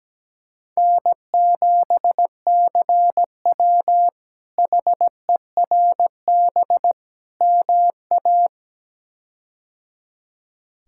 Morse code, heard as a keyed tone, about 17 wpm, 700 Hz. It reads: N7CW HERB MA
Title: K1USN SST QSO Practice - 17wpm